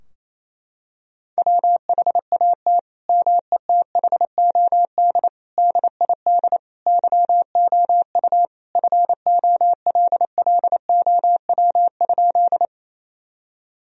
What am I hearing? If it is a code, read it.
Morse code, heard as a keyed tone, about 27 wpm, 700 Hz. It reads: WHAT METHOD DID YOU FOLLOW?
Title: Sentences from Top 1000 Words - Review Part 3 - 28wpm